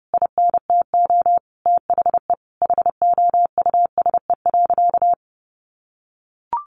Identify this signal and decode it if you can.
Morse code, heard as a keyed tone, about 30 wpm, 700 Hz. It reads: INTO THE HOUSE.
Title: Sentences from Top 1500 Words 30wpm